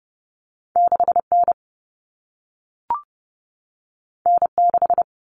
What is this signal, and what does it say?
Morse code, heard as a keyed tone, about 30 wpm, 700 Hz. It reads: 6N N6
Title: Common Cut Numbers - 2-Digit 30wpm